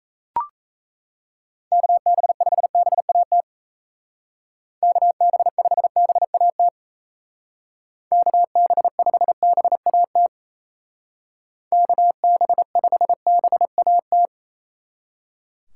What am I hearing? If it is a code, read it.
Morse code, heard as a keyed tone, about 33 wpm, 700 Hz. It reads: KB5BAT KB5BAT KB5BAT KB5BAT
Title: Speed Racing - USA Call Signs 28wpm